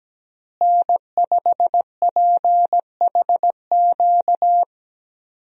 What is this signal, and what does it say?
Morse code, heard as a keyed tone, about 17 wpm, 700 Hz. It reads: N5PHQ
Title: USA Call Signs - 5 Characters - Repeated 2x - 17wpm